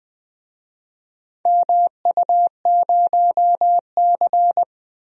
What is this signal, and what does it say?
Morse code, heard as a keyed tone, about 20 wpm, 700 Hz. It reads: MU0C
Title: DXCC-100 Call Signs - 4 Characters - 20wpm